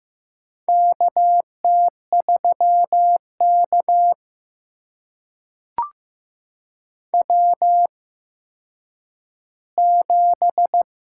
Morse code, heard as a keyed tone, about 15 wpm, spaced as for 9 wpm, 700 Hz. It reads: KT3K W7
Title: USA Call Signs - 4 Characters - 8x Character Spacing - 15wpm